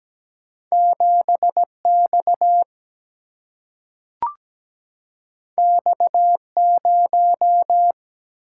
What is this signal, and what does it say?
Morse code, heard as a keyed tone, about 17 wpm, 700 Hz. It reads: 7 X X 0
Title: Single-Letter Single-Number 17wpm